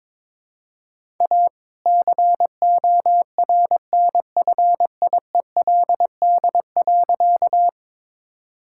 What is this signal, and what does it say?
Morse code, heard as a keyed tone, about 22 wpm, 700 Hz. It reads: A CORNFIELD.
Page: morse-code-ninja-practice-22wpm